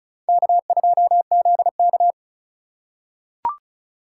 Morse code, heard as a keyed tone, about 34 wpm, 700 Hz. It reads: K2ZK
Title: USA Call Signs - 4 Characters - 3x Character Spacing - 35wpm